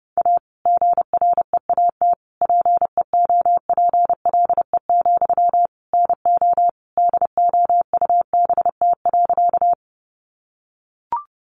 Morse code, heard as a keyed tone, about 30 wpm, 700 Hz. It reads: A GREAT PEOPLE, NO DOUBT.
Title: Sentences from Top 1500 Words 30wpm